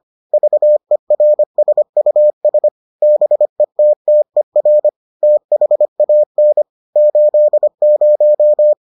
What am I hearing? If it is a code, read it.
Morse code, heard as a keyed tone, about 25 wpm, 600 Hz. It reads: VERSUS BETTER THAN 80